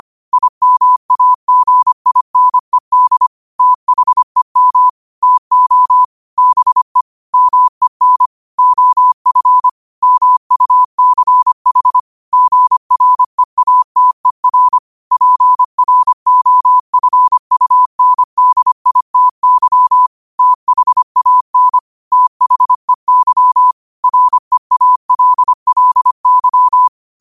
Morse code, heard as a keyed tone, about 25 wpm, 1000 Hz. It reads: IMAGINED THEM TO BE MEN OF MUCH GREATER PROFUNDITY THAN THEY REALLY
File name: qotdfiles / 25wpm